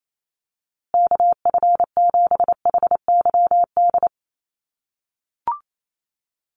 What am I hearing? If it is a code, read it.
Morse code, heard as a keyed tone, about 28 wpm, 700 Hz. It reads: KF7HYD
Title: USA Call Signs Repeated 2x 28wpm